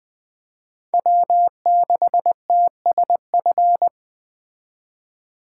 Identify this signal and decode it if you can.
Morse code, heard as a keyed tone, about 21 wpm, 700 Hz. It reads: W6TSF